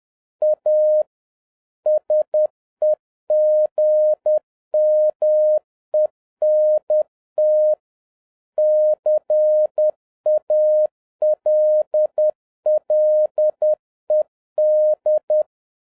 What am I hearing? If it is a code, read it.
Morse code, heard as a keyed tone, about 10 wpm, 610 Hz. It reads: A SEGMENT CALLED